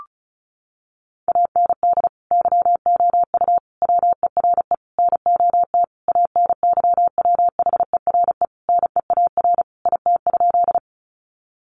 Morse code, heard as a keyed tone, about 34 wpm, 700 Hz. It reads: AND YOU WERE NOT ANYWHERE NEAR IT?
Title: Sentences from Top 1400 Words 35wpm